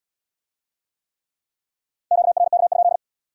Morse code, heard as a keyed tone, about 67 wpm, 700 Hz. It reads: QRK?